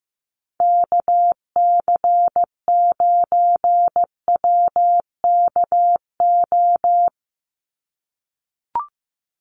Morse code, heard as a keyed tone, about 15 wpm, 700 Hz. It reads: KC9WKO
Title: USA Call Signs 15wpm